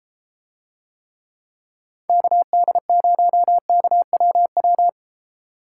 Morse code, heard as a keyed tone, about 32 wpm, 700 Hz. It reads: KD0KWW